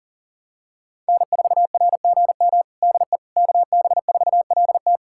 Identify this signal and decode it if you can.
Morse code, heard as a keyed tone, about 40 wpm, 700 Hz. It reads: N4RGM DE KB4LT
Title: USA Callsign DE Callsign - 5 Characters - 40wpm